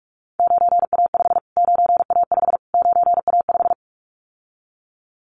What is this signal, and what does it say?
Morse code, heard as a keyed tone, about 46 wpm, 700 Hz. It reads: 9A5 9A5 9A5
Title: Callsign Prefixes for Top 25 DXCC - Repeated 3x - 45wpm